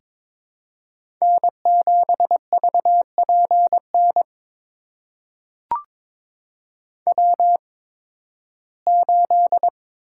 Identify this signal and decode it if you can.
Morse code, heard as a keyed tone, about 23 wpm, spaced as for 15 wpm, 700 Hz. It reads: N7VPN W8
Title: USA Call Signs - 5 Characters - 8x Character Spacing - 22wpm